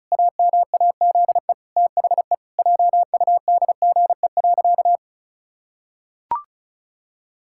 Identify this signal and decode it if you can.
Morse code, heard as a keyed tone, about 36 wpm, 700 Hz. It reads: AMAZE THE JUDGE.
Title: Pangram Challenge v2 - 35wpm